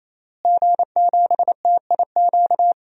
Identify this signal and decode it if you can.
Morse code, heard as a keyed tone, about 28 wpm, 700 Hz. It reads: G7TIQ
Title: DXCC-100 Call Signs - 5 Characters - Repeated 3x - 28wpm